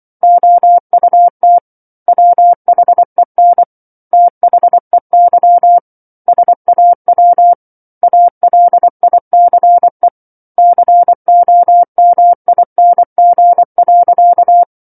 Morse code, heard as a keyed tone, about 24 wpm, 700 Hz. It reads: OUT WHEN THEY SAW ALICE COMING.
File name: 07-Chapter-07 24 WPM